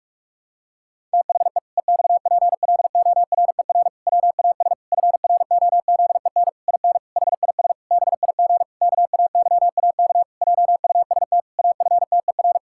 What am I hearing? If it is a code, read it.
Morse code, heard as a keyed tone, about 46 wpm, 700 Hz. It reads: THE EXPLORER WAS FROZEN IN HIS BIG KAYAK JUST AFTER
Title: Pangram Challenge v2 - 45wpm